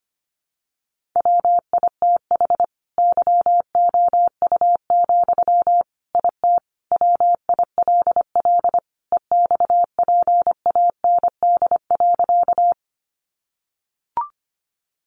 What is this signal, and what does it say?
Morse code, heard as a keyed tone, about 25 wpm, 700 Hz. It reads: WITH YOU, IT WILL EXPAND.
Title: Sentences from Top 1400 Words 25wpm